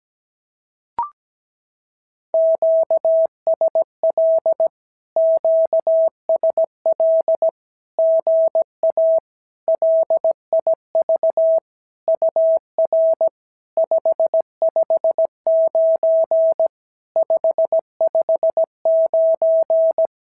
Morse code, heard as a keyed tone, about 17 wpm, 650 Hz. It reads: QSL QSL GA LIV UR 559 559
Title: SOTA QSO Practice 17wpm